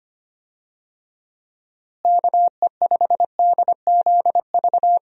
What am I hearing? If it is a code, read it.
Morse code, heard as a keyed tone, about 25 wpm, 700 Hz. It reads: KE5DZV
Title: USA Call Signs - 6 Characters - Repeated 2x - 25wpm